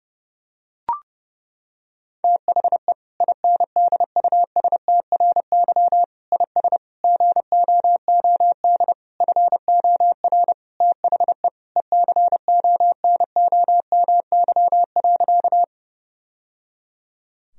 Morse code, heard as a keyed tone, about 30 wpm, 700 Hz. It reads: THE INDUSTRY IS GOOD FOR THE ECONOMY.
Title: Sentences from Top 1000 Words - Review Part 4 - 30wpm